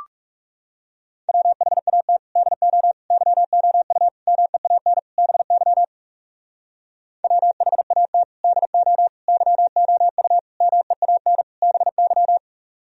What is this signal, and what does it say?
Morse code, heard as a keyed tone, about 40 wpm, 700 Hz. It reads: WHAT DO YOU MEAN BY WHAT DO YOU MEAN BY